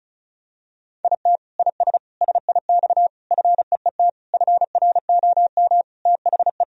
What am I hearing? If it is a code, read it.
Morse code, heard as a keyed tone, about 34 wpm, 700 Hz. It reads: IT IS SIX FEET FROM THE
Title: Sentences from Top 1000 Words - Review Part 2 - 35wpm